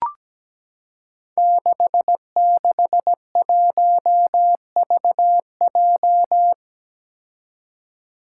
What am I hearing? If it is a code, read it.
Morse code, heard as a keyed tone, about 17 wpm, 700 Hz. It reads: 661VJ
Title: Five Challenging Letter-Number Combinations 17wpm